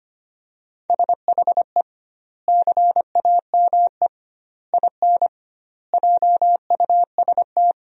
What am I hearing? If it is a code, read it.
Morse code, heard as a keyed tone, about 25 wpm, 700 Hz. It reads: SHE CAME IN JUST
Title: 2x Word-Spacing - Sentences from Top 300 Words - 25wpm